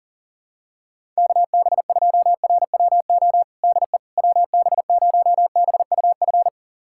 Morse code, heard as a keyed tone, about 40 wpm, 700 Hz. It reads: KB2RWO DE WB0BUF
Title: USA Callsign DE Callsign - 6 Characters - 40wpm